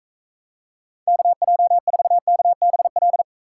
Morse code, heard as a keyed tone, about 36 wpm, 700 Hz. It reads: KJ4KBL